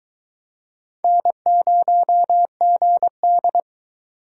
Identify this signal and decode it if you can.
Morse code, heard as a keyed tone, about 22 wpm, 700 Hz. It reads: N0GD